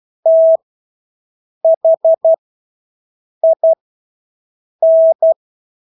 Morse code, heard as a keyed tone, about 12 wpm, spaced as for 6 wpm, 650 Hz. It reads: THIN
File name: Engelske ord 12-6 wpm